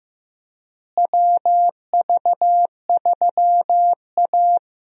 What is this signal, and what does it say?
Morse code, heard as a keyed tone, about 15 wpm, 700 Hz. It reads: WV3A
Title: USA Call Signs - 4 Characters - Repeated 3x - 15wpm